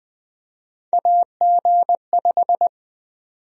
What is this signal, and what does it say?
Morse code, heard as a keyed tone, about 20 wpm, 700 Hz. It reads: AG5
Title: USA Call Sign Prefixes - 20wpm